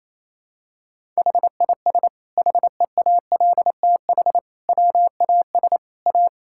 Morse code, heard as a keyed tone, about 28 wpm, 700 Hz. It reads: HIS HEALTH WAS A
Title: Sentences from Top 1600 Words 28wpm